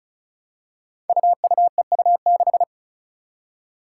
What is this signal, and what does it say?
Morse code, heard as a keyed tone, about 34 wpm, 700 Hz. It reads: UUEU6
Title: Five Challenging Letter-Number Combinations 35wpm